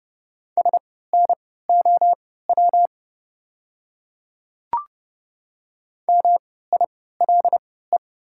Morse code, heard as a keyed tone, about 30 wpm, spaced as for 17 wpm, 690 Hz. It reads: SNOW MILE
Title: 3x Character Spacing - Top 500 Words - 30wpm